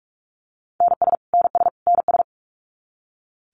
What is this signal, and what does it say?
Morse code, heard as a keyed tone, about 44 wpm, 700 Hz. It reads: NS NS NS
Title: Canadian Provinces and Territories - Repeated 3x - 45wpm